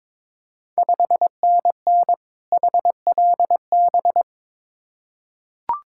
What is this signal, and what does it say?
Morse code, heard as a keyed tone, about 22 wpm, 690 Hz. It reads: 5NN HLB
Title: Ontario QSO Party Exchange 22wpm